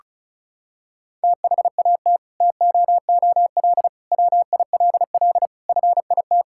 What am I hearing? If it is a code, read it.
Morse code, heard as a keyed tone, about 36 wpm, 700 Hz. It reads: THAT TOOL WILL FIT
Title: Sentences from Top 1000 Words - Review Part 6 - 35wpm